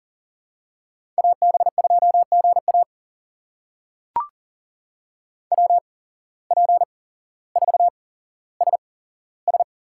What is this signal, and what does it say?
Morse code, heard as a keyed tone, about 39 wpm, spaced as for 17 wpm, 700 Hz. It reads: AB2GA WP4SS
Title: USA Call Signs - 5 Characters - 8x Character Spacing - 40wpm